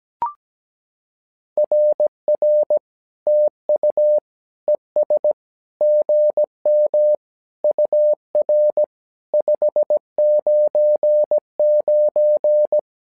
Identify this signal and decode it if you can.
Morse code, heard as a keyed tone, about 17 wpm, 600 Hz. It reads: RR TU ES GM UR 599
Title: SOTA QSO Practice 17wpm